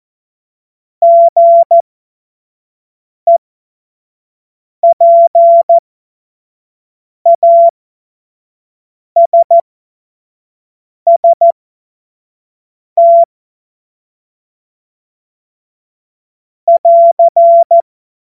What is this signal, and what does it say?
Morse code, heard as a keyed tone, about 14 wpm, spaced as for 5 wpm, 680 Hz. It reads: GEPASST <AR>
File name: Morsekurs / Witze / Witz 31. Mai 2022 25 F70